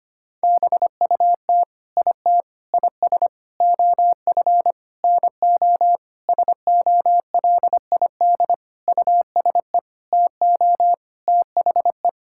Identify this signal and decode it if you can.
Morse code, heard as a keyed tone, about 25 wpm, 700 Hz. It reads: BUT IT IS OF NO SOLID USE TO THE
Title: Sentences from Top 1600 Words 25wpm